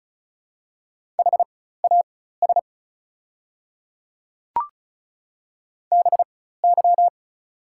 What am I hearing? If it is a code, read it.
Morse code, heard as a keyed tone, about 36 wpm, spaced as for 16 wpm, 700 Hz. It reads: HAS BY
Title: 4x Character Spacing - Top 100 Words - 35wpm